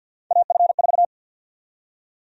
Morse code, heard as a keyed tone, about 52 wpm, 700 Hz. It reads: AF4
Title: USA Call Sign Prefixes - 50wpm